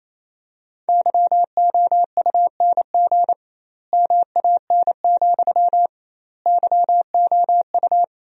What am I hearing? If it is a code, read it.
Morse code, heard as a keyed tone, about 28 wpm, 700 Hz. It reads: YOUNG MAN, YOU
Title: 2x Word-Spacing - Sentences from Top 400 Words - 28wpm